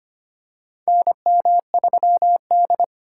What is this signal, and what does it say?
Morse code, heard as a keyed tone, about 25 wpm, 700 Hz. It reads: NM3D
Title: USA Call Signs - 4x Character Spacing - 25wpm